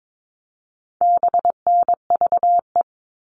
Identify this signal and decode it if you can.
Morse code, heard as a keyed tone, about 22 wpm, 700 Hz. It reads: BNVE
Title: All Cut Numbers - 4-Digit 22wpm